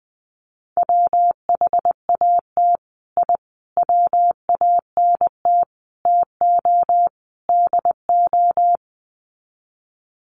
Morse code, heard as a keyed tone, about 20 wpm, 700 Hz. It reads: WHAT I WANT TO DO
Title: Sets of 5 Words Encore 20wpm